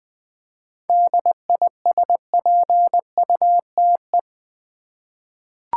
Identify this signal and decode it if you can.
Morse code, heard as a keyed tone, about 20 wpm, 700 Hz. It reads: DISPUTE
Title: 7 Letter Words - First 3 Letters Given - 20wpm